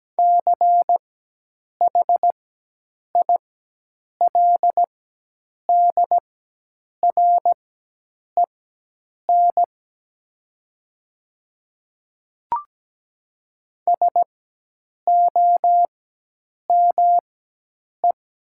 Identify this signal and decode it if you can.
Morse code, heard as a keyed tone, about 17 wpm, spaced as for 8 wpm, 700 Hz. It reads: CHILDREN SOME
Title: 4x Character Spacing - Top 300 Words - 17wpm